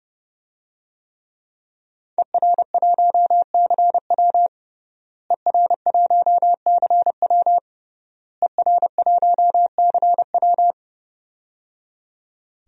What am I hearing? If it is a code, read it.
Morse code, heard as a keyed tone, about 30 wpm, 700 Hz. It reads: ER1CW ER1CW ER1CW
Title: DXCC-100 Call Signs - 5 Characters - Repeated 3x - 30wpm